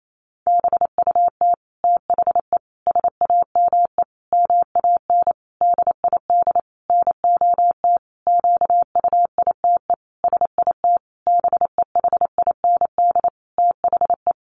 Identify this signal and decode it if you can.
Morse code, heard as a keyed tone, about 28 wpm, 700 Hz. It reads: BUT THE SAME MAN DID NOT QUITE SIT BEHIND THE
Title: A Sound Of Thunder 28wpm